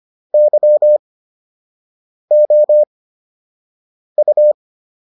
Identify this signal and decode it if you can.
Morse code, heard as a keyed tone, about 25 wpm, spaced as for 6 wpm, 600 Hz. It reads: YOU